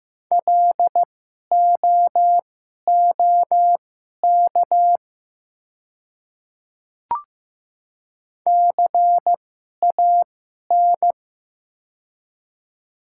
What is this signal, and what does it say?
Morse code, heard as a keyed tone, about 15 wpm, spaced as for 11 wpm, 700 Hz. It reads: LOOK CAN
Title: 2x Character Spacing - Top 100 Words - 15wpm